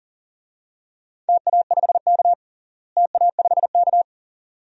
Morse code, heard as a keyed tone, about 40 wpm, 700 Hz. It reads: TA5K TA5K
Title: DXCC-100 Call Signs - 4 Characters - Repeated 2x - 40wpm